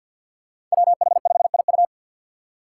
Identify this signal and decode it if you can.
Morse code, heard as a keyed tone, about 49 wpm, 700 Hz. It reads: WH5IU
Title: Five Challenging Letter-Number Combinations 50wpm